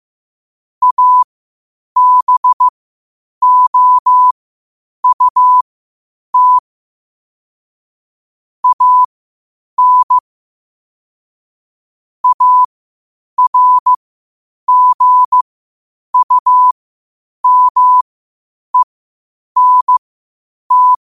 Morse code, heard as a keyed tone, about 15 wpm, spaced as for 8 wpm, 1000 Hz. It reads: ABOUT AN ARGUMENT
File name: qotdfiles / 8wpm